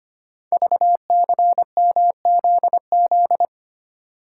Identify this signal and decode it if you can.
Morse code, heard as a keyed tone, about 25 wpm, 700 Hz. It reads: VCMZZ
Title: Code Groups - 5 Letters - 25wpm